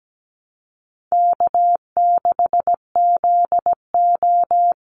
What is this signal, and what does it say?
Morse code, heard as a keyed tone, about 17 wpm, 700 Hz. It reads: K6ZO
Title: Easy Call Signs Repeated 3x - 17wpm